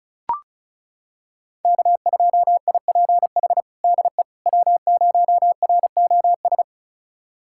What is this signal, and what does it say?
Morse code, heard as a keyed tone, about 36 wpm, 700 Hz. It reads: K2IPH DE W0ROS
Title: USA Callsign DE Callsign - 5 Characters - 35wpm